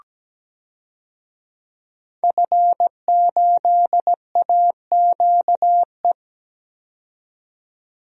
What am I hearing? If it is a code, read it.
Morse code, heard as a keyed tone, about 17 wpm, 700 Hz. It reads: F8AQE
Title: DXCC-100 Call Signs - 5 Characters - 17wpm